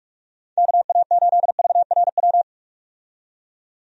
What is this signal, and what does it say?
Morse code, heard as a keyed tone, about 46 wpm, 700 Hz. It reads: KA8VRW
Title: USA Call Signs - 6 Characters - 45wpm